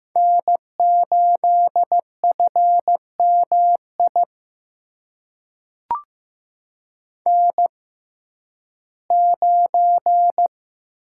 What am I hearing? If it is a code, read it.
Morse code, heard as a keyed tone, about 15 wpm, spaced as for 11 wpm, 700 Hz. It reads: N8FMI N9
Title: USA Call Signs - 5 Characters - 6x Character Spacing - 15wpm